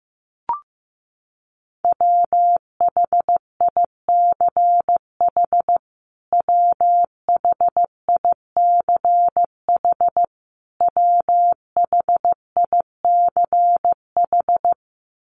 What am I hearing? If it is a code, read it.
Morse code, heard as a keyed tone, about 15 wpm, 700 Hz. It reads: WHICH WHICH WHICH
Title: Top 300 Words Repeated 15wpm